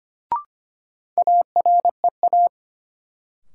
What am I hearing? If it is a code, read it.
Morse code, heard as a keyed tone, about 25 wpm, 700 Hz. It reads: AREA
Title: Sentences from Top 1000 Words - Review Part 2 - New Word Drill - 25wpm